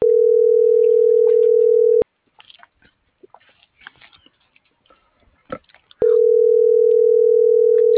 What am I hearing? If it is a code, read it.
Ringback tone (North America), 440 + 480 Hz.